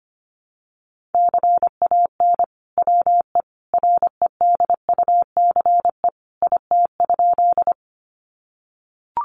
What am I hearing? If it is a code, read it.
Morse code, heard as a keyed tone, about 25 wpm, 700 Hz. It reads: CAN WE REDUCE IT?
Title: Sentences from Top 1100 Words 25wpm